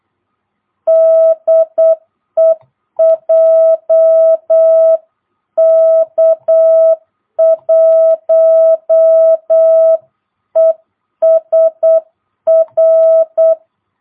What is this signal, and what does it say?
Morse code, heard as a keyed tone, about 8 wpm, 660 Hz. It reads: DE JK1ESR
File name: jh8cbh / 2022cw